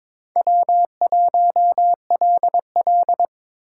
Morse code, heard as a keyed tone, about 22 wpm, 700 Hz. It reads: W1LL
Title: USA Call Signs - 4 Characters - 2x Character Spacing - 22wpm